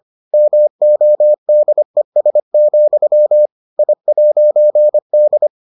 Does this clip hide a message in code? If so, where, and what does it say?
Yes: Morse code from 0.3 to 5.5 s, heard as a keyed tone, about 25 wpm, 600 Hz. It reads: MODES, I'D